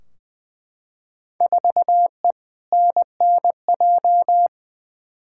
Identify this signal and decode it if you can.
Morse code, heard as a keyed tone, about 20 wpm, 700 Hz. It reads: 4E NNJ
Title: Field Day Exchanges - 20wpm